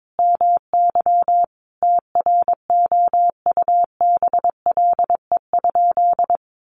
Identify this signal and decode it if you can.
Morse code, heard as a keyed tone, about 22 wpm, 700 Hz. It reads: MY TROUBLE?
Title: Sentences from Top 1200 Words 22wpm